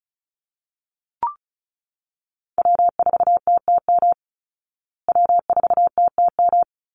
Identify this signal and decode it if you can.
Morse code, heard as a keyed tone, about 34 wpm, 700 Hz. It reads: W4TTM W4TTM
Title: Call Signs Repeated 3x - 35wpm